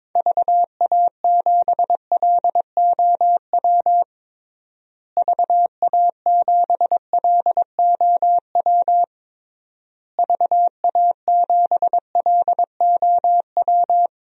Morse code, heard as a keyed tone, about 22 wpm, 700 Hz. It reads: VA7LOW VA7LOW VA7LOW
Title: DXCC-100 Call Signs - 6 Characters - Repeated 3x - 22wpm